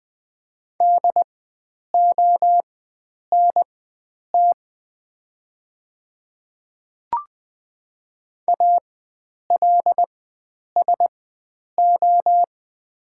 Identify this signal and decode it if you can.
Morse code, heard as a keyed tone, about 20 wpm, spaced as for 9 wpm, 700 Hz. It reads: DONT ALSO
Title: 4x Character Spacing - Top 500 Words - 20wpm